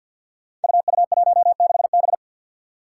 Morse code, heard as a keyed tone, about 50 wpm, 700 Hz. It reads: UU16B
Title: Five Challenging Letter-Number Combinations 50wpm